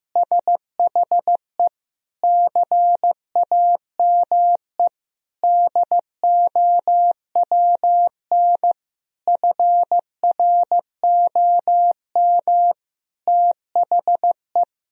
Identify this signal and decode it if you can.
Morse code, heard as a keyed tone, about 16 wpm, 700 Hz. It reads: SHE CAME DOWN FROM THE
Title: Sentences from Top 1000 Words - Review Part 1 - 15wpm